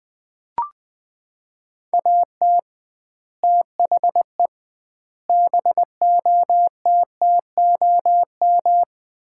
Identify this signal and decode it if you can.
Morse code, heard as a keyed tone, about 21 wpm, 700 Hz. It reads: AT THE BOTTOM
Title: 2x Word-Spacing - Sets of 5 Words - 20wpm